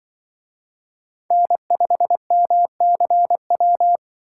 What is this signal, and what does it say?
Morse code, heard as a keyed tone, about 23 wpm, 700 Hz. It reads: N5MCW
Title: Speed Racing - USA Call Signs 20wpm